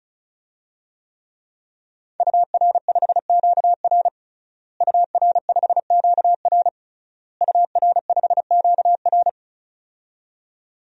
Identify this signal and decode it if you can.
Morse code, heard as a keyed tone, about 36 wpm, 700 Hz. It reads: UR5QR UR5QR UR5QR
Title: DXCC-100 Call Signs - 5 Characters - Repeated 3x - 35wpm